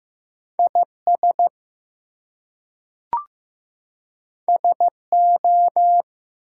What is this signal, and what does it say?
Morse code, heard as a keyed tone, about 15 wpm, 700 Hz. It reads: IS SO
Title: Two Letter Words - 15wpm